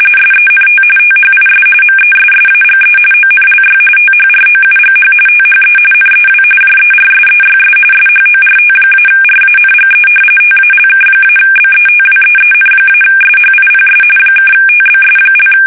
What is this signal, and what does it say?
RTTY, about 45.45 baud. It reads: SENT THIS MESSAGE ONLY THROUGH THE NEW JERSEYS SHORT-RANGE TALK-BETWEEN-SHIPS (TBS)
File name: fsk850 / battleshipnj